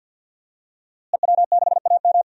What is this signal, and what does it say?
Morse code, heard as a keyed tone, about 52 wpm, 700 Hz. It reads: EW6AM
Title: DXCC-100 Call Signs - 5 Characters - Repeated 3x - 50wpm